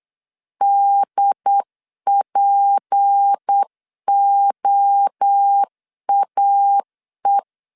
Morse code, heard as a keyed tone, about 8 wpm, 800 Hz. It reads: DPOAE
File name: LEZIONI / 7 GF9Z / 7 G 8wpm 3118